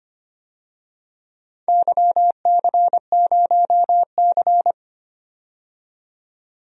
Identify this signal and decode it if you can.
Morse code, heard as a keyed tone, about 25 wpm, 700 Hz. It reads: YC0C
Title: DXCC-100 Call Signs - 4 Characters - 25wpm